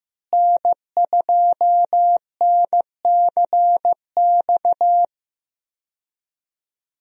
Morse code, heard as a keyed tone, about 15 wpm, 700 Hz. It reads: N2NCX
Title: USA Call Signs - 5 Characters - Repeated 2x - 15wpm